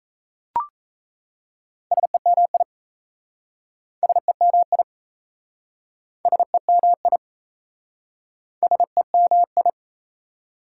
Morse code, heard as a keyed tone, about 37 wpm, 690 Hz. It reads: SEMI SEMI SEMI SEMI